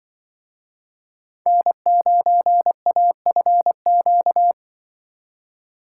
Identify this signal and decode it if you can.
Morse code, heard as a keyed tone, about 24 wpm, 700 Hz. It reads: N9AFQ
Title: Speed Racing - USA 1x3 Call Signs 20wpm